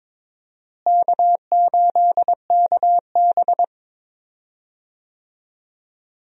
Morse code, heard as a keyed tone, about 22 wpm, 700 Hz. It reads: K8KB
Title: USA Call Signs - 4 Characters - Repeated 2x - 22wpm